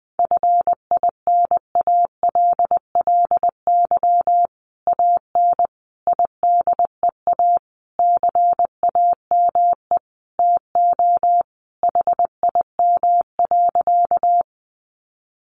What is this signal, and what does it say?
Morse code, heard as a keyed tone, about 20 wpm, 700 Hz. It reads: FINALLY AN IDEA CAME TO HIM.
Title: Sentences from Top 1000 Words 20wpm